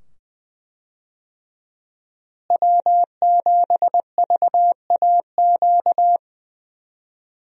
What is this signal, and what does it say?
Morse code, heard as a keyed tone, about 20 wpm, 700 Hz. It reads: W7VAQ
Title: USA Call Signs - 5 Characters - Repeated 2x - 20wpm